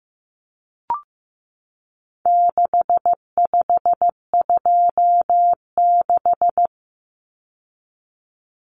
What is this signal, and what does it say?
Morse code, heard as a keyed tone, about 15 wpm, 700 Hz. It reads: 6526
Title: Numbers - 4-Digit 15wpm